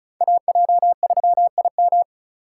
Morse code, heard as a keyed tone, about 34 wpm, 700 Hz. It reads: AJ3IM
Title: USA Call Signs - 5 Characters - 3x Character Spacing - 35wpm